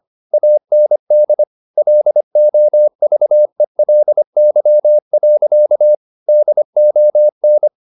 Morse code, heard as a keyed tone, about 25 wpm, 600 Hz. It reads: AND LOVELY. DON
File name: the old man and the sea part 1 25wpm